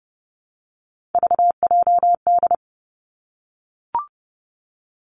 Morse code, heard as a keyed tone, about 30 wpm, 700 Hz. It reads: VJD
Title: Three Challenging Letter-Number Combinations 30wpm